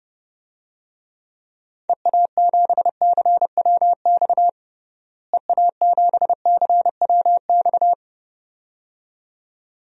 Morse code, heard as a keyed tone, about 32 wpm, 700 Hz. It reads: EA7CWX EA7CWX
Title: DXCC-100 Call Signs - 6 Characters - Repeated 2x - 30wpm